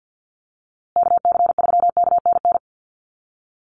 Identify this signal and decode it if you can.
Morse code, heard as a keyed tone, about 50 wpm, 700 Hz. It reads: KC3KNN
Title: USA Call Signs Repeated 3x 50wpm